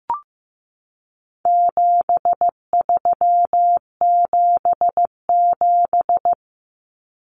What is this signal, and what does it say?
Morse code, heard as a keyed tone, about 15 wpm, 700 Hz. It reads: 7377
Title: Common Cut Numbers - 4-Digit 15wpm